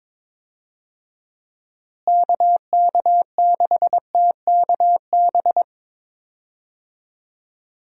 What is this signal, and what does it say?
Morse code, heard as a keyed tone, about 21 wpm, 700 Hz. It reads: KK6TKB
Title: USA Call Signs - 6 Characters - Repeated 3x - 22wpm